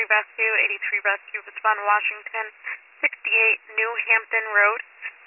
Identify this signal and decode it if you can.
Emergency Alert System (SAME) tones, heard as harsh buzzing bursts, with speech. It is heard from 0.8 to 4.6 s.